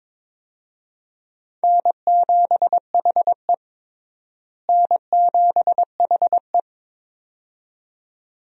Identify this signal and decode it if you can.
Morse code, heard as a keyed tone, about 22 wpm, 700 Hz. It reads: N7HE N7HE
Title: USA Call Signs - 4 Characters - Repeated 2x - 22wpm